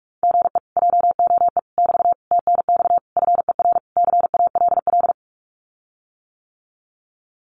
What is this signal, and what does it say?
Morse code, heard as a keyed tone, about 44 wpm, 700 Hz. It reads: GE JOE <BT> TNX FER CALL
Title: Common QSO Phrases Encore 45wpm